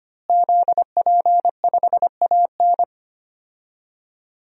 Morse code, heard as a keyed tone, about 25 wpm, 700 Hz. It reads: ZP5AN
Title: DXCC-100 Call Signs - 5 Characters - Repeated 3x - 25wpm